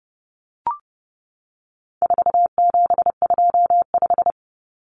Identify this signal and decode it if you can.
Morse code, heard as a keyed tone, about 30 wpm, 700 Hz. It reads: 4725
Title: Numbers - 4-Digit 30wpm